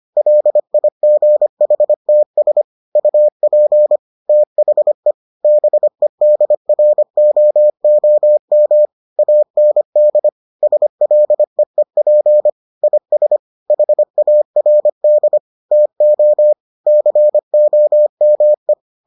Morse code, heard as a keyed tone, about 25 wpm, 600 Hz. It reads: LIGHTS UP THE BEDROOM AND SLEEP IS HARD TO COME